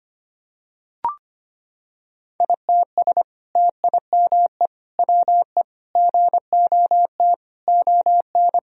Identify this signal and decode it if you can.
Morse code, heard as a keyed tone, about 25 wpm, 700 Hz. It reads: ITS TIME WE GOT ON
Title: Sentences from Top 1000 Words - Review Part 4 - 25wpm